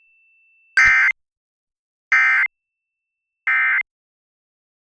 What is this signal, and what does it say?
Emergency Alert System (SAME) tones, heard as harsh buzzing bursts. It is heard from 0.7 to 3.8 s.